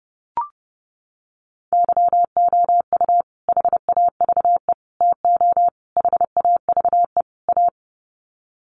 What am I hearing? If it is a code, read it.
Morse code, heard as a keyed tone, about 28 wpm, 700 Hz. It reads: YOU HAVE TO HAVE A
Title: Sets of 5 Words Encore 30wpm